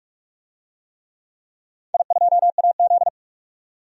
Morse code, heard as a keyed tone, about 42 wpm, 700 Hz. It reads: I2AZ
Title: DXCC-100 Call Signs - 4 Characters - 45wpm